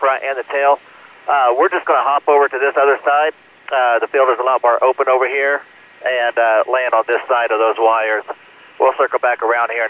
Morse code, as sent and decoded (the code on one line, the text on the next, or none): none